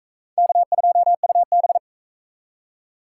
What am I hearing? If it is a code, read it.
Morse code, heard as a keyed tone, about 39 wpm, 700 Hz. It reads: K2UB